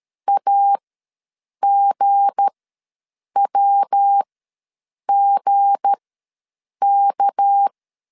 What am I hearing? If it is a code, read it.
Morse code, heard as a keyed tone, about 13 wpm, spaced as for 7 wpm, 800 Hz. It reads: AGWGK
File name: LEZIONI / 7 GF9Z / 7 G ws 12wpm 3140